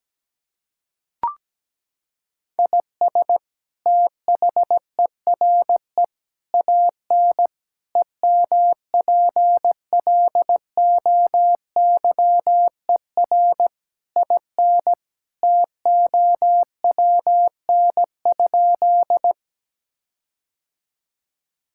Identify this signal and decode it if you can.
Morse code, heard as a keyed tone, about 17 wpm, 700 Hz. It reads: IS THERE AN EMPLOYER IN TOWN?
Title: Sentences from Top 1600 Words 17wpm